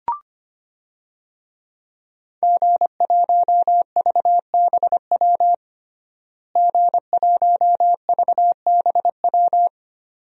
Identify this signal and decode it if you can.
Morse code, heard as a keyed tone, about 25 wpm, 700 Hz. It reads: G1VBW G1VBW
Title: DXCC-100 Call Signs - 5 Characters - Repeated 2x - 25wpm